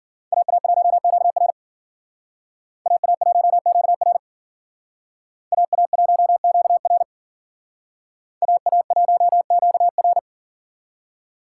Morse code, heard as a keyed tone, about 47 wpm, 700 Hz. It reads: AA1QR AA1QR AA1QR AA1QR
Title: Speed Racing - USA 2x2 Call Signs 40wpm